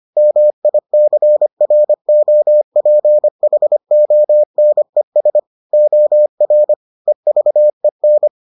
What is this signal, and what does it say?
Morse code, heard as a keyed tone, about 25 wpm, 600 Hz. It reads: MICROPHONES OR EVEN